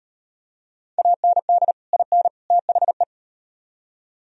Morse code, heard as a keyed tone, about 38 wpm, 700 Hz. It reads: AND IN THE